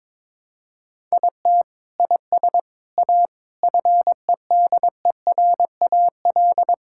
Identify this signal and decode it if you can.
Morse code, heard as a keyed tone, about 22 wpm, 700 Hz. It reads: IT IS A FEDERAL
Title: Sentences from Top 1000 Words - Review Part 5 - 22wpm